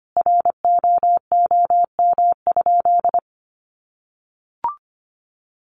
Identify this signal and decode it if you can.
Morse code, heard as a keyed tone, about 25 wpm, 700 Hz. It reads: ROOM?
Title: Sentences from Top 300 Words 25wpm